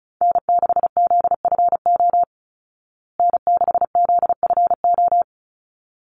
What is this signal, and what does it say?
Morse code, heard as a keyed tone, about 34 wpm, 700 Hz. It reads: N6ZFO N6ZFO
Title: Call Signs Repeated 3x - 35wpm